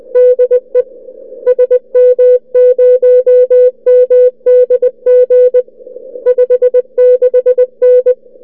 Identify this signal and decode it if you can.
Morse code, heard as a keyed tone, about 21 wpm, 500 Hz. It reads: DE SM0MDG 56N
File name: n6tr / n6tr qso sm0mdg @ oz1lxj 0230utc 16dec2008